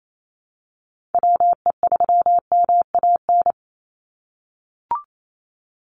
Morse code, heard as a keyed tone, about 28 wpm, 700 Hz. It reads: WE3MAN
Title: USA Call Signs 28wpm